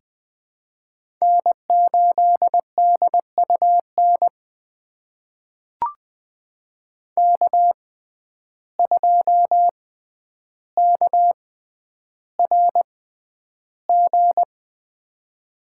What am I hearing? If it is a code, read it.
Morse code, heard as a keyed tone, about 20 wpm, spaced as for 10 wpm, 700 Hz. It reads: N8DUN K2KRG
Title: USA Call Signs - 5 Characters - 6x Character Spacing - 20wpm